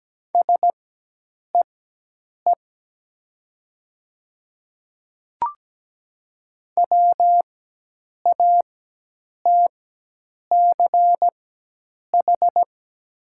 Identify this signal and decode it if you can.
Morse code, heard as a keyed tone, about 17 wpm, spaced as for 8 wpm, 690 Hz. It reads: SEE WATCH
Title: 4x Character Spacing - Top 500 Words - 17wpm